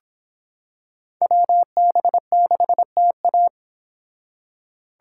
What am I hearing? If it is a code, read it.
Morse code, heard as a keyed tone, about 25 wpm, 700 Hz. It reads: WB6TA